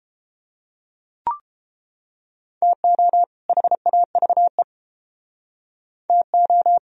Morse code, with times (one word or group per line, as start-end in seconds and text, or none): 2.6-3.2 TO
3.5-4.6 HAVE
6.1-6.8 TO